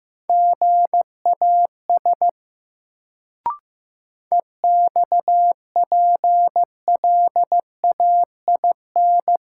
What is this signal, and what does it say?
Morse code, heard as a keyed tone, about 15 wpm, 700 Hz. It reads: GAS EXPLAIN
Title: Sentences from Top 1100 Words - New Word Drill Alt - 15wpm